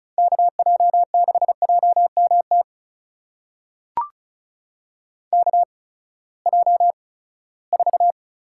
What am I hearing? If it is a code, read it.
Morse code, heard as a keyed tone, about 36 wpm, spaced as for 20 wpm, 700 Hz. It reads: KJ6JMT KJ4
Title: USA Call Signs - 6 Characters - 8x Character Spacing - 35wpm